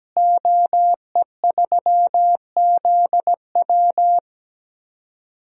Morse code, heard as a keyed tone, about 17 wpm, 700 Hz. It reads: OE3ZW
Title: DXCC-100 Call Signs - 5 Characters - Repeated 3x - 17wpm